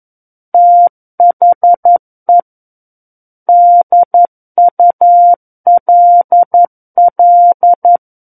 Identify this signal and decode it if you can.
Morse code, heard as a keyed tone, about 11 wpm, 700 Hz. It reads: THE DULL
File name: 11 WPM